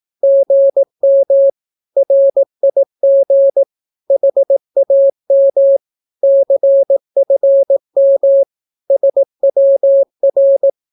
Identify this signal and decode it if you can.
Morse code, heard as a keyed tone, about 18 wpm, 550 Hz. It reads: GM RIG HAM CFM SWR
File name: naukapodstaw / skroty / skroty